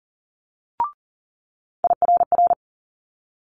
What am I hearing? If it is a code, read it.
Morse code, heard as a keyed tone, about 38 wpm, 700 Hz. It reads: IRR
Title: Multi-letter Phonemes Quick Fire 40wpm